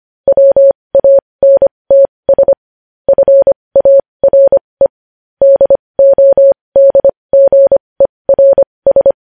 Morse code, heard as a keyed tone, about 25 wpm, spaced as for 20 wpm, 550 Hz. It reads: WANTS FARE DODGERS
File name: podcast / News headlines-1x-25x20